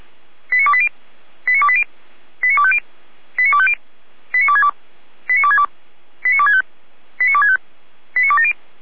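Five-tone selective call (ZVEI), heard as quick runs of five tones, tones of about 70 ms. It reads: 89280 89280 89250 89250 89262 89262 89265 89265 89270
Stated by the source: ZVEI five-tone call 89246 received by SDR on channel 455.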